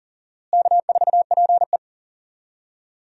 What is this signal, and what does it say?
Morse code, heard as a keyed tone, about 40 wpm, 700 Hz. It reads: K4PE
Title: USA Call Signs - 4 Characters - Repeated 3x - 40wpm